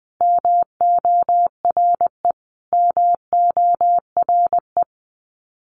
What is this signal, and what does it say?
Morse code, heard as a keyed tone, about 20 wpm, 700 Hz. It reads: MORE MORE
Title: Top 200 Words Repeated 20wpm